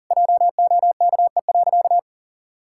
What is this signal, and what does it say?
Morse code, heard as a keyed tone, about 40 wpm, 700 Hz. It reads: JOKE.
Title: Pangram Challenge v2 - 40wpm